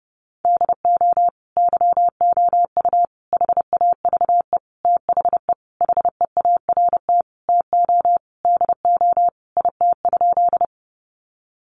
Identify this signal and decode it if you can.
Morse code, heard as a keyed tone, about 30 wpm, 700 Hz. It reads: DO YOU HAVE THE HEART TO DO IT?
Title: Sentences from Top 1200 Words 30wpm